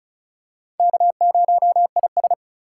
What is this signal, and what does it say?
Morse code, heard as a keyed tone, about 36 wpm, 700 Hz. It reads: K0IS
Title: USA Call Signs - 4 Characters - 2x Character Spacing - 35wpm